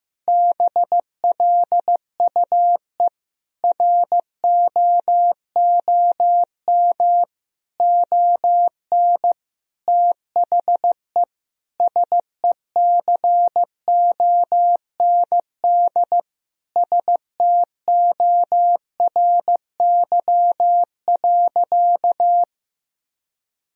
Morse code, heard as a keyed tone, about 15 wpm, 700 Hz. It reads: BLUE ROOM ON THE SECOND STORY.
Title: Sentences from Top 1000 Words - Review Part 3 - 15wpm